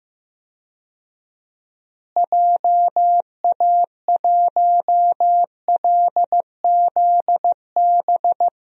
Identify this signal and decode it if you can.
Morse code, heard as a keyed tone, about 15 wpm, 700 Hz. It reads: JA1LZB
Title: DXCC-100 Call Signs - 6 Characters - Repeated 2x - 15wpm